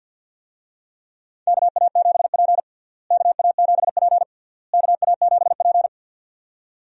Morse code, heard as a keyed tone, about 52 wpm, 700 Hz. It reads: KA7P KA7P KA7P
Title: USA Call Signs - 4 Characters - Repeated 3x - 50wpm